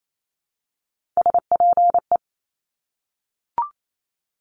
Morse code, heard as a keyed tone, about 28 wpm, 700 Hz. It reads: SPE
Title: Top Three Letter Combinations in Words 28wpm